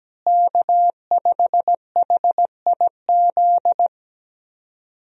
Morse code, heard as a keyed tone, about 17 wpm, 700 Hz. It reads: K5HIZ
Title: USA Call Signs - 5 Characters - Repeated 2x - 17wpm